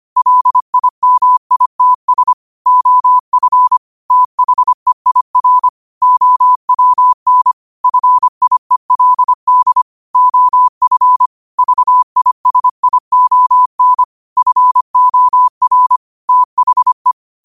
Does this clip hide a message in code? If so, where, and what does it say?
Yes: Morse code from 0.2 to 17.1 s, heard as a keyed tone, about 25 wpm, 1000 Hz. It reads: LIMITS OF THEIR OWN FIELD OF VISION FOR THE